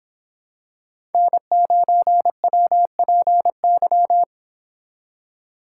Morse code, heard as a keyed tone, about 26 wpm, 700 Hz. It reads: N9WPY